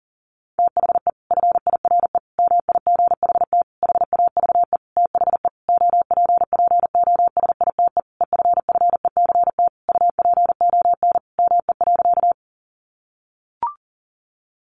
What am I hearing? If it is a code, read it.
Morse code, heard as a keyed tone, about 40 wpm, 700 Hz. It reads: THE FIRE MIGHT HAVE THE OPPOSITE EFFECT UPON ME.
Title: Sentences from Top 1300 Words 40wpm